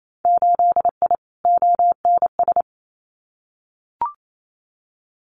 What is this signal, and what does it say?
Morse code, heard as a keyed tone, about 28 wpm, 700 Hz. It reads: 8I ONS
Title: Winter Field Day Exchanges Encore 28wpm